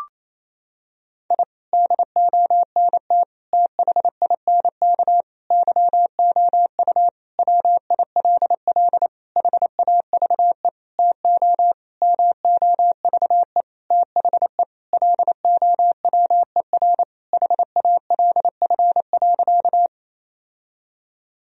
Morse code, heard as a keyed tone, about 28 wpm, 700 Hz. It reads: I DONT THINK YOU WILL HAVE TO MOVE THE LOWER HALF.
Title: Sentences from Top 1000 Words - Review Part 4 - 28wpm